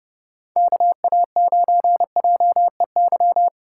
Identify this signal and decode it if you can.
Morse code, heard as a keyed tone, about 30 wpm, 700 Hz. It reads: KA9JEY